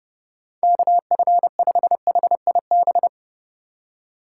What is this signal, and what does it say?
Morse code, heard as a keyed tone, about 30 wpm, 700 Hz. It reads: KF5HIB
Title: USA Call Signs - 2x Character Spacing - 30wpm